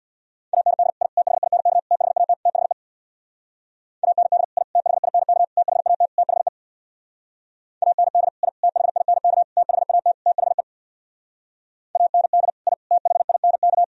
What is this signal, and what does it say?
Morse code, heard as a keyed tone, about 64 wpm, 690 Hz. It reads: AND I THINK THAT THE AND I THINK THAT THE AND I THINK THAT THE AND I THINK